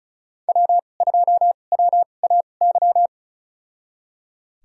Morse code, heard as a keyed tone, about 35 wpm, spaced as for 25 wpm, 690 Hz. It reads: W2WAY
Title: USA Call Signs - 5 Characters - 2x Character Spacing - 35wpm